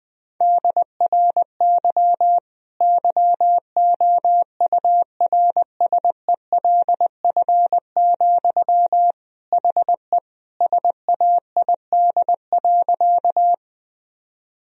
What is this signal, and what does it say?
Morse code, heard as a keyed tone, about 21 wpm, 700 Hz. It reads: DRY YOURSELF, HE SAID.
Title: Sentences from Top 1000 Words - Review Part 3 - 20wpm